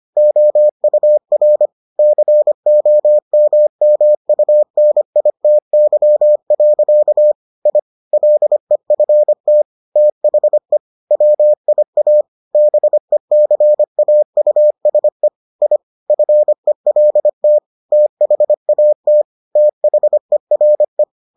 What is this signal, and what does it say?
Morse code, heard as a keyed tone, about 25 wpm, 600 Hz. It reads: OUR COMMUNITY. I LEFT THE WIA BECAUSE I FELT THAT THERE